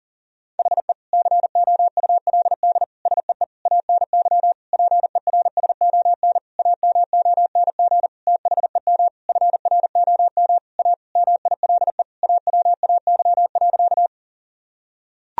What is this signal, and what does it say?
Morse code, heard as a keyed tone, about 40 wpm, 700 Hz. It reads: HE COULD SEE ANY PERSON AMONG THEM FROM A MILE AWAY.
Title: Sentences from Top 1000 Words - Review Part 3 - 40wpm